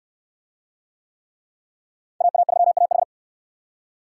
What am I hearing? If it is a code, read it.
Morse code, heard as a keyed tone, about 69 wpm, 700 Hz. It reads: AA3AH